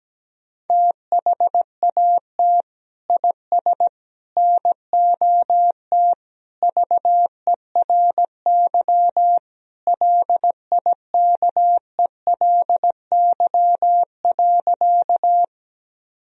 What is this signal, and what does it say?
Morse code, heard as a keyed tone, about 17 wpm, 700 Hz. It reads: THAT IS NOT VERY LIKELY.
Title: Sentences from Top 1000 Words - Review Part 4 - 17wpm